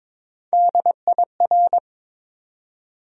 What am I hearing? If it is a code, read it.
Morse code, heard as a keyed tone, about 22 wpm, 700 Hz. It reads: DIR
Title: USA Call Sign Postfixes - 3 Letters 22wpm